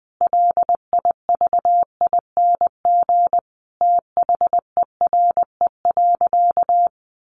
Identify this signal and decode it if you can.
Morse code, heard as a keyed tone, about 20 wpm, 700 Hz. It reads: LIVING THERE.
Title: Sentences from Top 1500 Words 20wpm